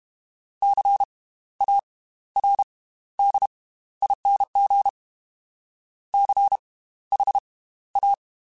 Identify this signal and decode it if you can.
Morse code, heard as a keyed tone, about 32 wpm, spaced as for 15 wpm, 800 Hz. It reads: CARDING CHA
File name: Slow ing @30